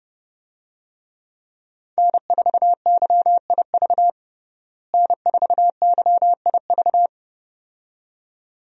Morse code, heard as a keyed tone, about 30 wpm, 700 Hz. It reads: N4YIV N4YIV
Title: USA Call Signs - 5 Characters - Repeated 2x - 30wpm